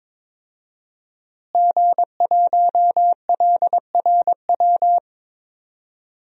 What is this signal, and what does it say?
Morse code, heard as a keyed tone, about 21 wpm, 700 Hz. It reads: G1LRW
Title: DXCC-100 Call Signs - 5 Characters - Repeated 3x - 22wpm